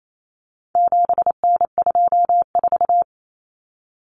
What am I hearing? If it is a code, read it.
Morse code, heard as a keyed tone, about 28 wpm, 700 Hz. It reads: 7N24
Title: Common Cut Numbers - 4-Digit 28wpm